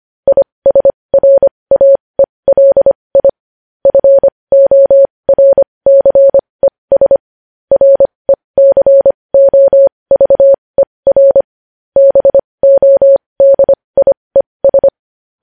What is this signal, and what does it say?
Morse code, heard as a keyed tone, about 25 wpm, spaced as for 20 wpm, 550 Hz. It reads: ISRAELI FORCES RECOVER BODIES
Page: News Headlines in Morse Code at 20 WPM